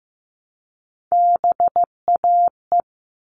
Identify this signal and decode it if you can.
Morse code, heard as a keyed tone, about 15 wpm, 700 Hz. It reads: BAE
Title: All Cut Numbers - 3-Digit 15wpm